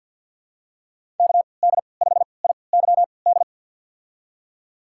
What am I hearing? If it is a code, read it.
Morse code, heard as a keyed tone, about 52 wpm, spaced as for 29 wpm, 700 Hz. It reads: KD5IYD
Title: USA Call Signs - 6 Characters - 3x Character Spacing - 50wpm